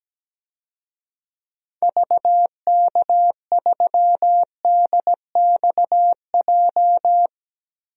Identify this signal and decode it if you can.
Morse code, heard as a keyed tone, about 17 wpm, 700 Hz. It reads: VK3DXJ
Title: DXCC-100 Call Signs - 6 Characters - 17wpm